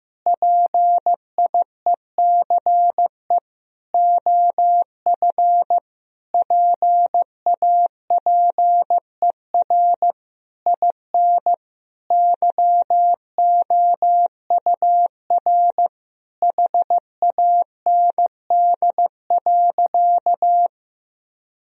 Morse code, heard as a keyed tone, about 16 wpm, 700 Hz. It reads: PIECE OF PAPER IN YOUR HAND.
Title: Sentences from Top 1000 Words - Review Part 2 - 15wpm